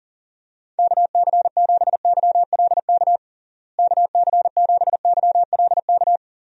Morse code, heard as a keyed tone, about 40 wpm, 700 Hz. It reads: KC7YLK KC7YLK
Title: USA Call Signs - 6 Characters - Repeated 2x - 40wpm